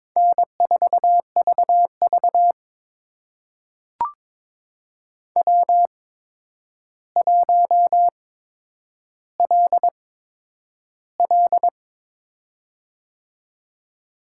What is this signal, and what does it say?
Morse code, heard as a keyed tone, about 22 wpm, spaced as for 9 wpm, 700 Hz. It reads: N4VV W1LL
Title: USA Call Signs - 4 Characters - 8x Character Spacing - 22wpm